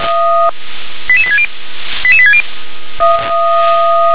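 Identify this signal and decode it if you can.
Five-tone selective call (ZVEI), heard as quick runs of five tones, tones of about 70 ms. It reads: 88860 88860
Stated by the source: SDR-received ZVEI five-tone selective call, code 88863.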